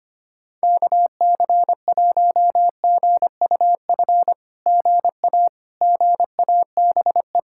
Morse code, heard as a keyed tone, about 26 wpm, 700 Hz. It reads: KC1GUF GA GABE
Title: SOTA QSO Practice 25wpm